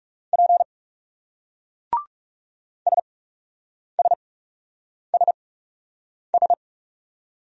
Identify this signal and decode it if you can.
Morse code, heard as a keyed tone, about 38 wpm, spaced as for 8 wpm, 700 Hz. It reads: P SSSS